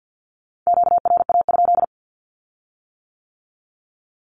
Morse code, heard as a keyed tone, about 50 wpm, 700 Hz. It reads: QRA?
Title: Q-Codes with Spoken Meaning - 50wpm